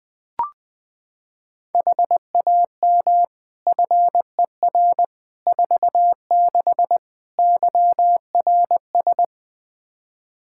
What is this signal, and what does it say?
Morse code, heard as a keyed tone, about 21 wpm, 700 Hz. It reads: HAM FER 46 YRS
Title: Practice QSO - Exchange 3 - More Info - 20wpm